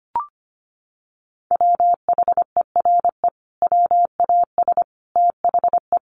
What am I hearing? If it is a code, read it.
Morse code, heard as a keyed tone, about 25 wpm, 700 Hz. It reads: WHERE WAS THE
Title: Sentences from Top 1000 Words 25wpm